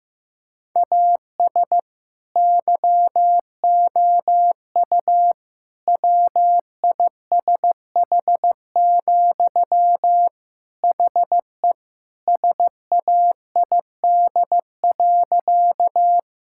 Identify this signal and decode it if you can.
Morse code, heard as a keyed tone, about 15 wpm, 700 Hz. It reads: AS YOU WISH, HE SAID.
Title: Sentences from Top 1600 Words 15wpm